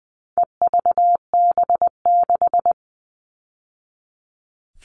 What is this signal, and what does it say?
Morse code, heard as a keyed tone, about 20 wpm, 700 Hz. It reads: EVB6
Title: All Cut Numbers - 4-Digit 20wpm